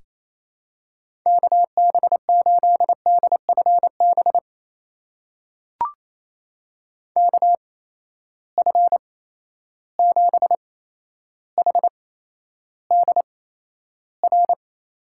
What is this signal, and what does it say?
Morse code, heard as a keyed tone, about 28 wpm, spaced as for 12 wpm, 700 Hz. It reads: KB8DFB KF7HDR
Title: USA Call Signs - 6 Characters - 8x Character Spacing - 28wpm